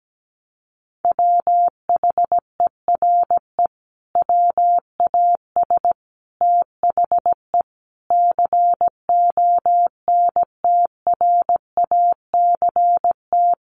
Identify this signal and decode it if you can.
Morse code, heard as a keyed tone, about 17 wpm, 700 Hz. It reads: WHERE WAS THE CONTRACT
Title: Sentences from Top 1400 Words 17wpm